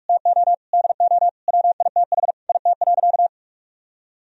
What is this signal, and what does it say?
Morse code, heard as a keyed tone, about 46 wpm, 700 Hz. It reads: TO DO WITH IT.
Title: Sentences from Top 1000 Words - Review Part 1 - 45wpm